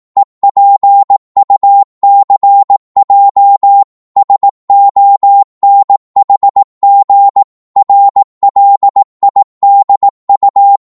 Morse code, heard as a keyed tone, about 18 wpm, 800 Hz. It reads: EPUCJ SONHG RLIDU
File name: Morse / ltr-18wpm